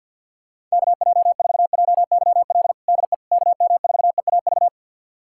Morse code, heard as a keyed tone, about 46 wpm, 700 Hz. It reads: KJ4JYL DE KM4EAV
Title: USA Callsign DE Callsign 50wpm